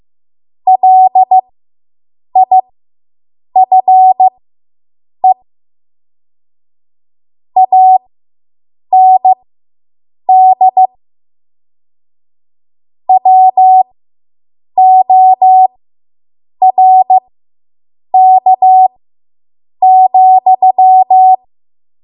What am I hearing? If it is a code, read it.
Morse code, heard as a keyed tone, about 15 wpm, spaced as for 7 wpm, 750 Hz. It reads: LIFE AND WORK,